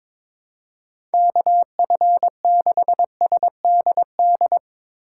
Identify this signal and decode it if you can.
Morse code, heard as a keyed tone, about 22 wpm, 700 Hz. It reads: KF6SDD
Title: USA Call Signs - 3x Character Spacing - 22wpm